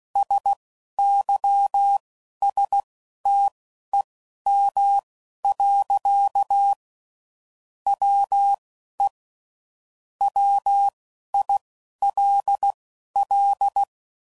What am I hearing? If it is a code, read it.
Morse code, heard as a keyed tone, about 16 wpm, spaced as for 12 wpm, 800 Hz. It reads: SYSTEM. WE WILL
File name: CW Code Groups @10WPM / Technical Subjects / SwPwrSup